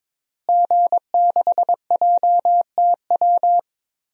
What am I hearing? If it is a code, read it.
Morse code, heard as a keyed tone, about 22 wpm, 700 Hz. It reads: G6JTW
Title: DXCC-100 Call Signs - 5 Characters - Repeated 3x - 22wpm